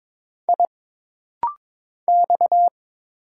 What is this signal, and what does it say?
Morse code, heard as a keyed tone, about 22 wpm, 700 Hz. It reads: I X
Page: This rapid-fire timing is geared to push your ICR (Instant Character Recognition) skills.